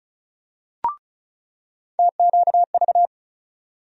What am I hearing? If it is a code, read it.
Morse code, heard as a keyed tone, about 36 wpm, 700 Hz. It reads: TQV
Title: USA Call Sign Postfixes - 3 Letters 35wpm